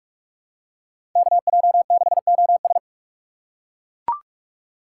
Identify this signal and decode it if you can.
Morse code, heard as a keyed tone, about 42 wpm, 700 Hz. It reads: KJ6OS
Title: USA Call Signs - 5 Characters - 2x Character Spacing - 45wpm